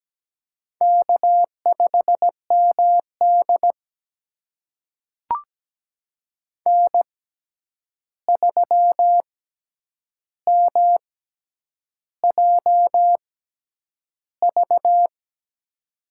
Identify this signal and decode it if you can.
Morse code, heard as a keyed tone, about 17 wpm, spaced as for 8 wpm, 700 Hz. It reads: K5MD N3MJV
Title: USA Call Signs - 6x Character Spacing - 17wpm